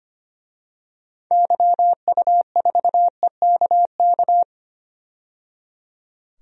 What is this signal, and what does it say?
Morse code, heard as a keyed tone, about 25 wpm, 700 Hz. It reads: YU4EKK
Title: DXCC-100 Call Signs - 6 Characters - 25wpm